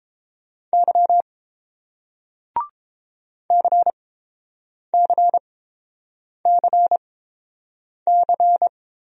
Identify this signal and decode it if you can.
Morse code, heard as a keyed tone, about 30 wpm, spaced as for 7 wpm, 700 Hz. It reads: Y CCCC